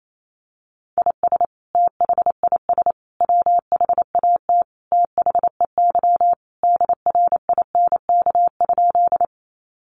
Morse code, heard as a keyed tone, about 28 wpm, 700 Hz. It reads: IS THIS WHAT THEY DRINK?
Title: Sentences from Top 1500 Words 28wpm